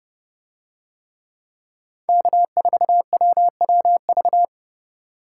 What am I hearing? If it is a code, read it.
Morse code, heard as a keyed tone, about 28 wpm, 700 Hz. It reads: K4WWV